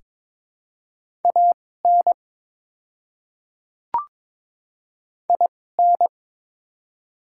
Morse code, heard as a keyed tone, about 22 wpm, spaced as for 16 wpm, 700 Hz. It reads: AN IN
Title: 2x Character Spacing - Top 200 Words - 22wpm